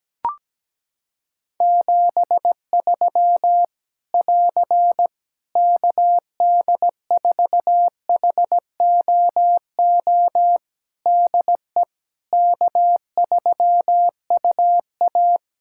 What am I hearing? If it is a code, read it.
Morse code, heard as a keyed tone, about 17 wpm, 700 Hz. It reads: 73 <AR> KD4HOO DE K3UA
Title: Practice QSO - Exchange 4 - Ending - 17wpm